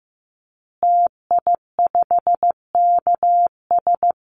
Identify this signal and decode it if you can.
Morse code, heard as a keyed tone, about 15 wpm, 700 Hz. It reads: TI5KS
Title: Call Signs Repeated 2x - 15wpm